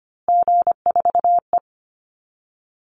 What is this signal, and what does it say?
Morse code, heard as a keyed tone, about 25 wpm, 700 Hz. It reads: G4E
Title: Call Signs Repeated 2x - 25wpm